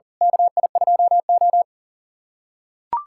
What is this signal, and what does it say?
Morse code, heard as a keyed tone, about 40 wpm, 700 Hz. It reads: KI2O
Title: USA Callsign DE Callsign - 4 Characters - 40wpm